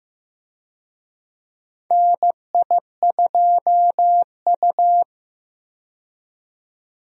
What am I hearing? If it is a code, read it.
Morse code, heard as a keyed tone, about 15 wpm, 700 Hz. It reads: NI2U
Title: USA Call Signs - 4 Characters - 15wpm